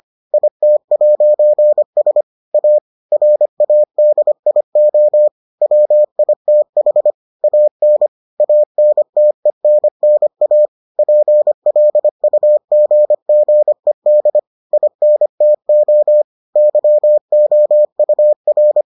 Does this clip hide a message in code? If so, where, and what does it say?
Yes: Morse code from 0.3 to 18.8 s, heard as a keyed tone, about 25 wpm, 600 Hz. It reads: IT'S A RADIO WITH AN ANTENNA PLUGGED INTO YOUR